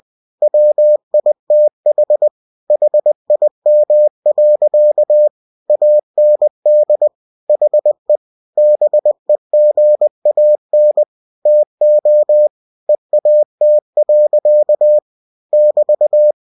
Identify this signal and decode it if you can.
Morse code, heard as a keyed tone, about 20 wpm, 600 Hz. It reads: WITH HIM. AND HE BEGAN TO EAT. <BT>